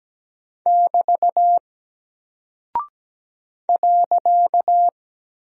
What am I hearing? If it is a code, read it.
Morse code, heard as a keyed tone, about 17 wpm, 700 Hz. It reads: <BT> .
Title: Single Character (Letter:Number:Punctuation:Prosign) Alt 17wpm